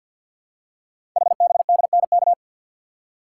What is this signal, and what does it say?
Morse code, heard as a keyed tone, about 52 wpm, 700 Hz. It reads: HBDNK
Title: Code Groups - 5 Letters - 50wpm